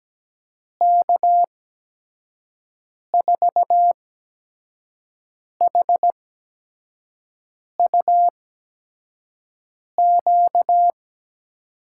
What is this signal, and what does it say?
Morse code, heard as a keyed tone, about 17 wpm, spaced as for 5 wpm, 700 Hz. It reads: K4HUQ
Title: USA Call Signs - 6 Characters - 8x Character Spacing - 17wpm